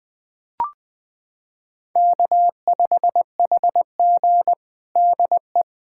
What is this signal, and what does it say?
Morse code, heard as a keyed tone, about 21 wpm, 700 Hz. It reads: K5HG DE
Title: USA Callsign DE Callsign - 4 Characters - 20wpm